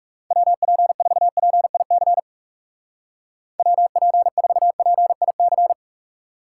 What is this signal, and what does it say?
Morse code, heard as a keyed tone, about 36 wpm, 700 Hz. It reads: WP4PIC WP4PIC
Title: Speed Racing - USA 2x3 Call Signs 30wpm